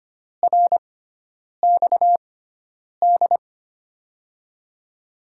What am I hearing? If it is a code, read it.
Morse code, heard as a keyed tone, about 25 wpm, spaced as for 9 wpm, 700 Hz. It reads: RXD
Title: USA Call Signs - 5 Characters - 6x Character Spacing - 25wpm